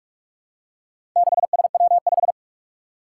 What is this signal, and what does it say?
Morse code, heard as a keyed tone, about 44 wpm, 700 Hz. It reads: 6SW5
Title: Four Challenging Letter-Number Combinations 45wpm